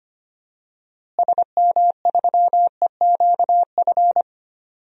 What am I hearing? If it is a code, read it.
Morse code, heard as a keyed tone, about 25 wpm, 700 Hz. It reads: SM3EQF
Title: DXCC-100 Call Signs - 6 Characters - Repeated 3x - 25wpm